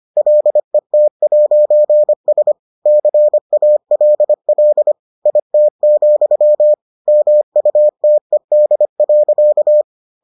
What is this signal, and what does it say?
Morse code, heard as a keyed tone, about 25 wpm, 600 Hz. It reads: LET'S CALL IT, MUTED.